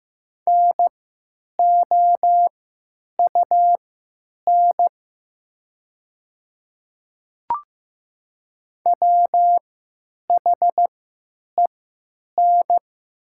Morse code, heard as a keyed tone, about 15 wpm, spaced as for 8 wpm, 700 Hz. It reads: NOUN WHEN
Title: 3x Character Spacing - Top 500 Words - 15wpm